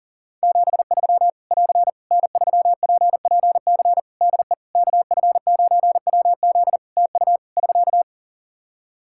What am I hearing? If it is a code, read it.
Morse code, heard as a keyed tone, about 40 wpm, 700 Hz. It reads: 73 <AR> N3PPC DE KF9WZ TU <SK>
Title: K1USN SST QSO Practice 40wpm